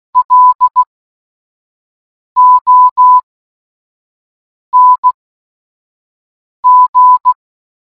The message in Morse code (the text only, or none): LONG